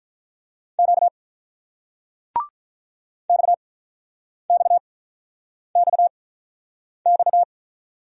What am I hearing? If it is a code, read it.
Morse code, heard as a keyed tone, about 50 wpm, spaced as for 9 wpm, 700 Hz. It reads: Q XXXX